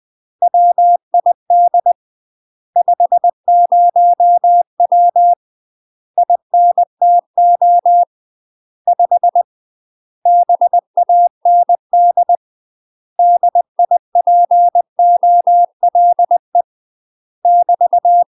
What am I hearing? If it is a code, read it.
Morse code, heard as a keyed tone, about 20 wpm, 700 Hz. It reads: WID 50W INTO 5 BAND DIPOLE <BT>